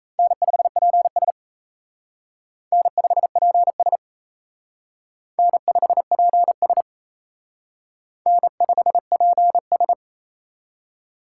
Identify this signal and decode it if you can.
Morse code, heard as a keyed tone, about 34 wpm, 700 Hz. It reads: N5PS N5PS N5PS N5PS
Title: Speed Racing - USA 1x2 Call Signs 28wpm